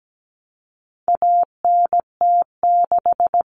Morse code, heard as a keyed tone, about 17 wpm, 700 Hz. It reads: ANT6
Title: Common Cut Numbers - 4-Digit 17wpm